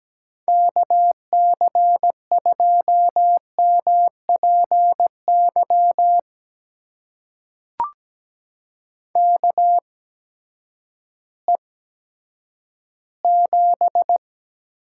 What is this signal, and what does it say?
Morse code, heard as a keyed tone, about 17 wpm, spaced as for 10 wpm, 700 Hz. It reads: KC2MPY KE7
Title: USA Call Signs - 6 Characters - 8x Character Spacing - 17wpm